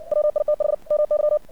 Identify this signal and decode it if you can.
Morse code, heard as a keyed tone, about 60 wpm, 600 Hz. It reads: WITH MY
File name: archived / CRSnet